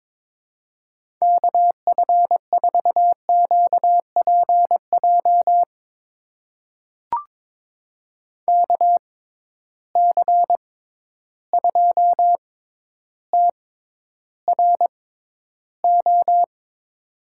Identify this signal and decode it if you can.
Morse code, heard as a keyed tone, about 22 wpm, spaced as for 11 wpm, 700 Hz. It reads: KF4QPJ KC2TRO
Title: USA Call Signs - 6 Characters - 6x Character Spacing - 22wpm